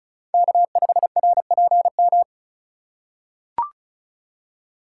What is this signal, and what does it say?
Morse code, heard as a keyed tone, about 34 wpm, 700 Hz. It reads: K5RPM
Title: USA Call Signs - 5 Characters - 3x Character Spacing - 35wpm